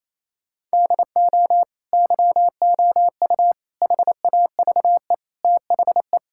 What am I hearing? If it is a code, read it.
Morse code, heard as a keyed tone, about 28 wpm, 700 Hz. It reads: DO YOU HAVE THE
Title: Sentences from Top 1000 Words - Review Part 5 - 28wpm